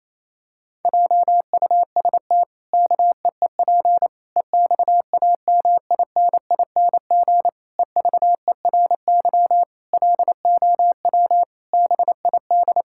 Morse code, heard as a keyed tone, about 28 wpm, 700 Hz. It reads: JUST KEEP EXAMINING EVERY LOW BID
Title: Pangram Challenge v2 - 28wpm